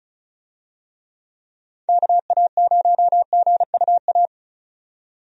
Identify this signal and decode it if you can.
Morse code, heard as a keyed tone, about 34 wpm, 700 Hz. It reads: KA0GUA
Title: USA Call Signs - 6 Characters - 35wpm